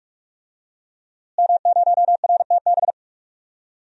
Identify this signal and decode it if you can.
Morse code, heard as a keyed tone, about 46 wpm, 700 Hz. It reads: M0RTB
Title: DXCC-100 Call Signs - 5 Characters - Repeated 2x - 45wpm